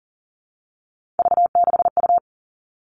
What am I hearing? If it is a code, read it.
Morse code, heard as a keyed tone, about 40 wpm, 700 Hz. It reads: V6U
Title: Three Challenging Letter-Number Combinations 40wpm